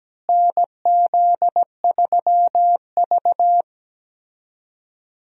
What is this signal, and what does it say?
Morse code, heard as a keyed tone, about 17 wpm, 700 Hz. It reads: NZ3V
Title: USA Call Signs - 4 Characters - Repeated 3x - 17wpm